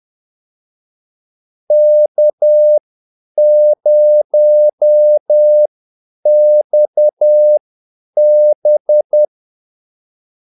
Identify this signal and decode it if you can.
Morse code, heard as a keyed tone, about 10 wpm, spaced as for 8 wpm, 600 Hz. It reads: K0XB
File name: Callsigns-10-8wpm